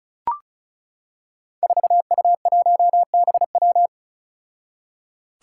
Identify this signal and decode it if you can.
Morse code, heard as a keyed tone, about 36 wpm, 700 Hz. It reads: 4U1BW
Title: Five Challenging Letter-Number Combinations 35wpm